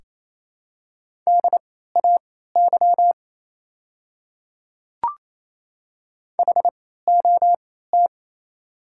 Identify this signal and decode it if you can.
Morse code, heard as a keyed tone, about 28 wpm, spaced as for 16 wpm, 700 Hz. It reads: DAY HOT
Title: 3x Character Spacing - Top 100 Words - 28wpm